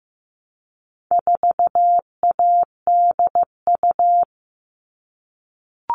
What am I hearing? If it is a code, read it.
Morse code, heard as a keyed tone, about 15 wpm, 700 Hz. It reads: 4ADU
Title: All Cut Numbers - 4-Digit 15wpm